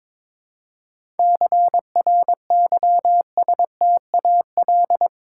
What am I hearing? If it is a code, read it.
Morse code, heard as a keyed tone, about 22 wpm, 690 Hz. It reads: CRYSTAL
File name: Difficult Words with Hints - 22wpm